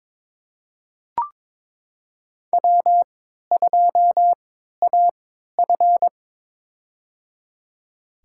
Morse code, heard as a keyed tone, about 22 wpm, spaced as for 13 wpm, 700 Hz. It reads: W2AF
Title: USA Call Signs - 3x Character Spacing - 22wpm